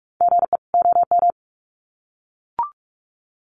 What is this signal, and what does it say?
Morse code, heard as a keyed tone, about 44 wpm, 700 Hz. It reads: GE OM
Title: Common QSO Phrases 45wpm